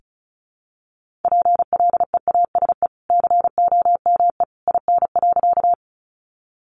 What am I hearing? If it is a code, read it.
Morse code, heard as a keyed tone, about 34 wpm, 700 Hz. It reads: PLEASE COME IN.
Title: Sentences from Top 1500 Words 35wpm